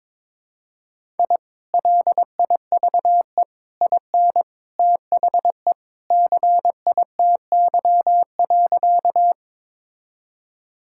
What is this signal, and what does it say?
Morse code, heard as a keyed tone, about 22 wpm, 700 Hz. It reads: I LIVE IN THE CITY.
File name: practice / 22wpm / Sentences with 2-Word Phrases - 22wpm